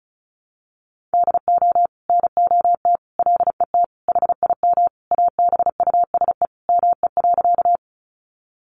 Morse code, heard as a keyed tone, about 34 wpm, 700 Hz. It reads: DO NOT LET HIM ABUSE ME.
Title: Sentences from Top 1300 Words 35wpm